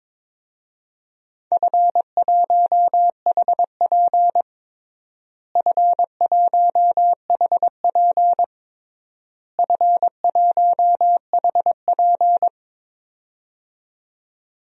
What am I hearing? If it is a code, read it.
Morse code, heard as a keyed tone, about 22 wpm, 700 Hz. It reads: F1HP F1HP F1HP
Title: DXCC-100 Call Signs - 4 Characters - Repeated 3x - 22wpm